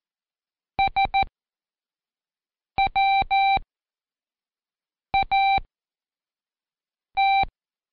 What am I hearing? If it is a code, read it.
Morse code, heard as a keyed tone, about 14 wpm, spaced as for 5 wpm, 780 Hz. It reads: SWAT
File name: Code Quick / AudioDownld1 / Seg18-Day2-Quiz ASYO